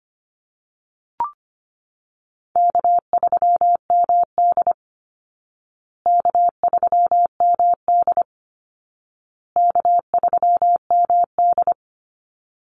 Morse code, heard as a keyed tone, about 25 wpm, 700 Hz. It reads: K3MD K3MD K3MD
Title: Easy Call Signs Repeated 3x - 25wpm